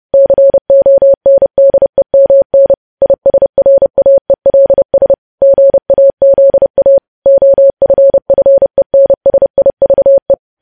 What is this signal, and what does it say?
Morse code, heard as a keyed tone, about 30 wpm, 550 Hz. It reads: CONDEMN ISRAELS GAZA OFFENSIVE
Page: news headlines in morse code at 30 wpm